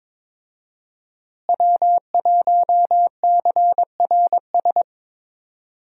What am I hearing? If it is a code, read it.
Morse code, heard as a keyed tone, about 22 wpm, 700 Hz. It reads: W1CRS
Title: USA Call Signs - 5 Characters - Repeated 3x - 22wpm